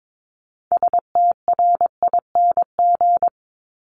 Morse code, heard as a keyed tone, about 22 wpm, 700 Hz. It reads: STRING
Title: Multi-letter Phonemes 22wpm